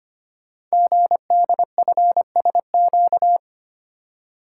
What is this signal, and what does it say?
Morse code, heard as a keyed tone, about 25 wpm, 700 Hz. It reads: GDFSQ
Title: Code Groups - 5 Letters - 25wpm